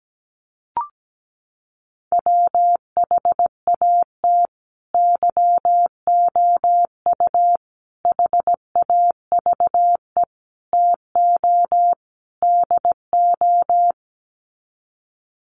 Morse code, heard as a keyed tone, about 17 wpm, 690 Hz. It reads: WHAT YOU HAVE TO DO
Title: Sets of 5 Words Encore 17wpm